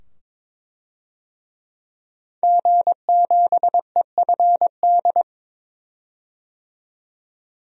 Morse code, heard as a keyed tone, about 22 wpm, 700 Hz. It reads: G7EFD
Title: DXCC-100 Call Signs - 5 Characters - 22wpm